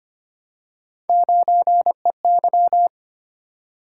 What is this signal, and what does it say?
Morse code, heard as a keyed tone, about 25 wpm, 700 Hz. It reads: 9EY
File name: Three Letter-Number Combinations - 25wpm